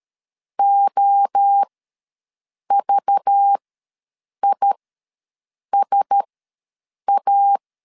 Morse code, heard as a keyed tone, about 13 wpm, spaced as for 7 wpm, 800 Hz. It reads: OVISA
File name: LEZIONI / 5 LBVC / 5 LBVC ws 12wpm 3159